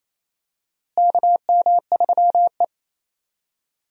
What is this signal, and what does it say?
Morse code, heard as a keyed tone, about 28 wpm, 700 Hz. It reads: KM3E
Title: USA Call Signs - 4 Characters - Repeated 3x - 28wpm